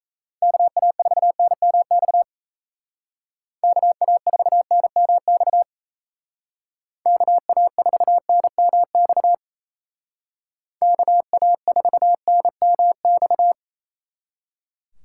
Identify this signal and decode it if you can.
Morse code, heard as a keyed tone, about 36 wpm, 700 Hz. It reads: KA4NMX KA4NMX KA4NMX KA4NMX